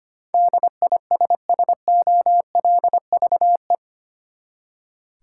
Morse code, heard as a keyed tone, about 25 wpm, 700 Hz. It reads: DISSOLVE
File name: Eight Letter Words - 25wpm